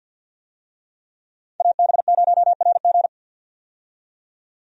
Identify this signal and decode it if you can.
Morse code, heard as a keyed tone, about 51 wpm, 700 Hz. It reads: AB0RG
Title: USA Call Signs - 5 Characters - 50wpm